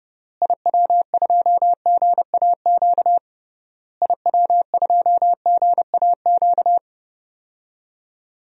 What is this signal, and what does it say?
Morse code, heard as a keyed tone, about 30 wpm, 700 Hz. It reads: IW2GAQ IW2GAQ
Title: DXCC-100 Call Signs - 6 Characters - Repeated 3x - 30wpm